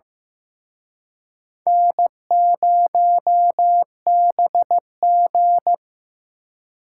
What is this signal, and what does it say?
Morse code, heard as a keyed tone, about 18 wpm, 700 Hz. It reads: N0BG
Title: Speed Racing - USA 1x2 Call Signs 15wpm